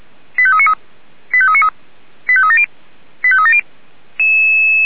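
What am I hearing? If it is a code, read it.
Five-tone selective call (ZVEI), heard as quick runs of five tones, tones of about 70 ms. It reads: 85393 85393 85370 85370
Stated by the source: Five-tone ZVEI selective call, address 89484, received on an SDR.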